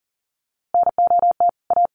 Morse code, heard as a keyed tone, about 40 wpm, 700 Hz. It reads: NOT A
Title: A Sound Of Thunder 40wpm